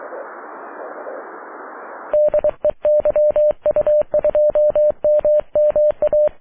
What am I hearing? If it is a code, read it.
Morse code, heard as a keyed tone, about 23 wpm, 600 Hz. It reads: DE YU2MMA
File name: wp / la8aja yu2mma aurora 05 aug 2011